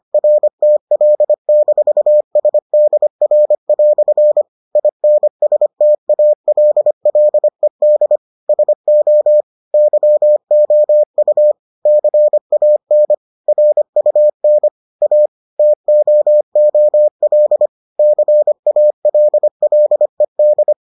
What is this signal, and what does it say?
Morse code, heard as a keyed tone, about 25 wpm, 600 Hz. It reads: RTL-SDR" INSTALLED SO YOU CAN RUN A TOOL CALLED